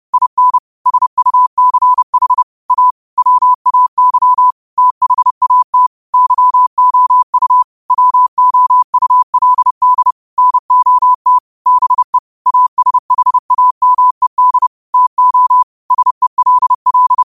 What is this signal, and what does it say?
Morse code, heard as a keyed tone, about 30 wpm, 1000 Hz. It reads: IN SUCH A WAY THAT YOU WOULD NOT BE ASHAMED TO SELL
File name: qotdfiles / 30wpm